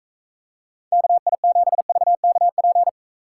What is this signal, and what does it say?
Morse code, heard as a keyed tone, about 45 wpm, 700 Hz. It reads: KI7VKP